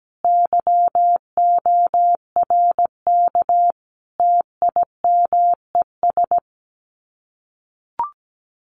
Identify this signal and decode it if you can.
Morse code, heard as a keyed tone, about 17 wpm, 700 Hz. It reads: YORK TIMES
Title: Sets of 5 Words 17wpm